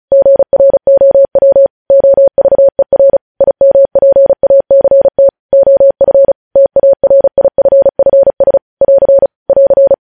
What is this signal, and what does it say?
Morse code, heard as a keyed tone, about 36 wpm, 550 Hz. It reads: GROW OVER IMPACT OF TARIFFS <AR> <AR>
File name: podcast / News headlines-1x-35x35